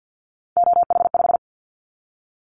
Morse code, heard as a keyed tone, about 50 wpm, 700 Hz. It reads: OH5
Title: Callsign Prefixes for Top 50 DXCC - 50wpm